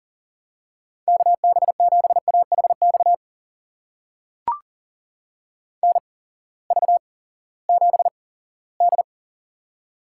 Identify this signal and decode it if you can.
Morse code, heard as a keyed tone, about 39 wpm, spaced as for 20 wpm, 700 Hz. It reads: KB7AHX NV7D
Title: USA Call Signs - 8x Character Spacing - 40wpm